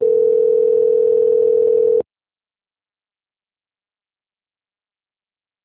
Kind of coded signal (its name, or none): telephone call progress tone